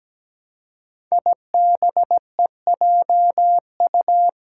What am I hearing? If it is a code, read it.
Morse code, heard as a keyed tone, about 17 wpm, 700 Hz. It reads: IBEJU
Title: Five Challenging Letter-Number Combinations 17wpm